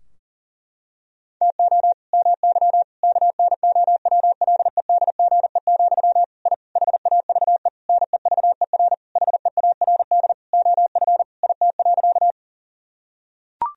Morse code, heard as a keyed tone, about 40 wpm, 700 Hz. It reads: TO MY KNOWLEDGE, I HAVE NEVER HEARD OF IT.
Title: Sentences from Top 1000 Words - Review Part 3 - 40wpm